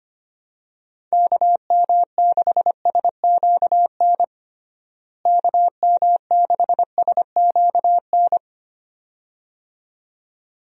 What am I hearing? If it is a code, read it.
Morse code, heard as a keyed tone, about 25 wpm, 700 Hz. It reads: KM6SQN KM6SQN
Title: USA Call Signs - 6 Characters - Repeated 2x - 25wpm